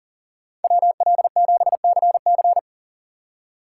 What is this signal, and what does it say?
Morse code, heard as a keyed tone, about 40 wpm, 700 Hz. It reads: WL7CC
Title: USA Call Signs - 5 Characters - 3x Character Spacing - 40wpm